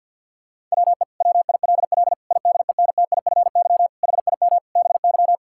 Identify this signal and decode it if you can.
Morse code, heard as a keyed tone, about 50 wpm, 700 Hz. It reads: WE WILL IDENTIFY HIM BY
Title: Sentences from Top 1000 Words - Review Part 6 - 50wpm